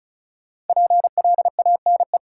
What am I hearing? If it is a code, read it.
Morse code, heard as a keyed tone, about 36 wpm, 700 Hz. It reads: PLANE
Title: Sentences from Top 1000 Words - Review Part 3 - New Word Drill - 35wpm